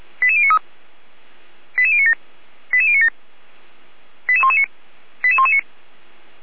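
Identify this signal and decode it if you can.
Five-tone selective call (ZVEI), heard as quick runs of five tones, tones of about 70 ms. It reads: 88093 88097 88097 88119 88119
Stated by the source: ZVEI five-tone call 88057 received by SDR.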